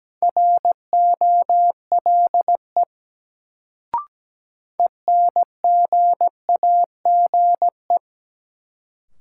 Morse code, heard as a keyed tone, about 17 wpm, 700 Hz. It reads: ROLE ENGAGE
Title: Sentences from Top 1500 Words - New Word Drill Alt - 17wpm